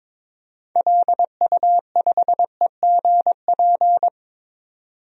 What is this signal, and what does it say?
Morse code, heard as a keyed tone, about 22 wpm, 700 Hz. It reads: LU5EGP
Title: DXCC-100 Call Signs - 6 Characters - 22wpm